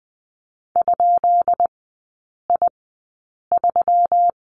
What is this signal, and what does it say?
Morse code, heard as a keyed tone, about 20 wpm, spaced as for 8 wpm, 700 Hz. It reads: ?I3
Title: 152 Lesson 25e - Letter p - Three Characters 20wpm